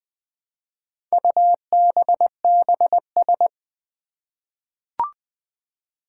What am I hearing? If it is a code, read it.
Morse code, heard as a keyed tone, about 20 wpm, 700 Hz. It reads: UBBS
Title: Four Challenging Letter-Number Combinations 20wpm